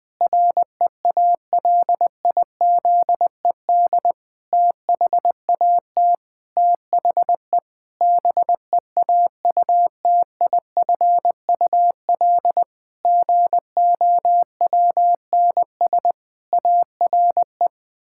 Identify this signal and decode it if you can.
Morse code, heard as a keyed tone, about 20 wpm, 700 Hz. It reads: REALIZED THAT THE BEAUTIFUL GOWNS ARE
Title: Pangram Challenge v2 - 20wpm